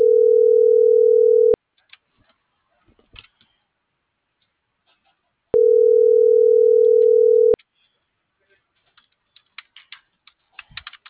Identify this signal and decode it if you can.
Ringback tone (North America), 440 + 480 Hz.
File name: CENTRAL BLN / out marco.llanos